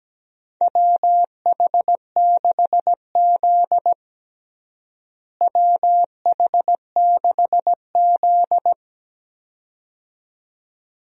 Morse code, heard as a keyed tone, about 17 wpm, 700 Hz. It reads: WH6Z WH6Z
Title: USA Call Signs - 4 Characters - Repeated 2x - 17wpm